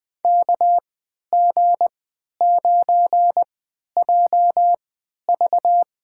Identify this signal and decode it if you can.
Morse code, heard as a keyed tone, about 20 wpm, spaced as for 11 wpm, 700 Hz. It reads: KG9JV
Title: USA Call Signs - 5 Characters - 3x Character Spacing - 20wpm